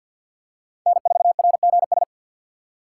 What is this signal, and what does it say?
Morse code, heard as a keyed tone, about 49 wpm, 700 Hz. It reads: N4RGS
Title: USA Call Signs - 5 Characters - 4x Character Spacing - 50wpm